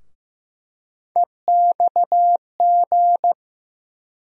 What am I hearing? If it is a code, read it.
Morse code, heard as a keyed tone, about 15 wpm, 700 Hz. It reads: EXG
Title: USA Call Sign Postfixes - 3 Letters 15wpm